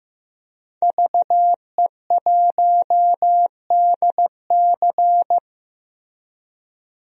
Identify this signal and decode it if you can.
Morse code, heard as a keyed tone, about 15 wpm, 700 Hz. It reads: VE1DC
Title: DXCC-100 Call Signs - 5 Characters - 15wpm